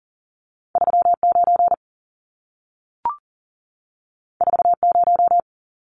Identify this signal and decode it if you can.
Morse code, heard as a keyed tone, about 40 wpm, 700 Hz. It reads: 39 40
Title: Numbers - 2-Digit 40wpm